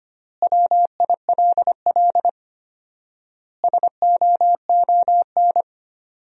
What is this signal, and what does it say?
Morse code, heard as a keyed tone, about 25 wpm, 700 Hz. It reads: WILL SOON
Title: 4x Word-Spacing - Sentences from Top 400 Words - 25wpm